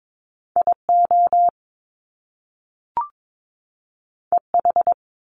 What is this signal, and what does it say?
Morse code, heard as a keyed tone, about 22 wpm, 700 Hz. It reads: IO EH
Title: Top Two Letter Combinations in Words 22wpm